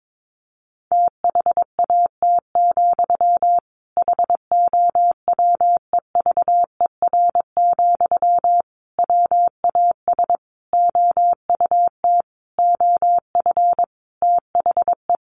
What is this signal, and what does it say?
Morse code, heard as a keyed tone, about 22 wpm, 700 Hz. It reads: THAT, HOWEVER, WAS OUT OF THE
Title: Sentences from Top 600 Words 22wpm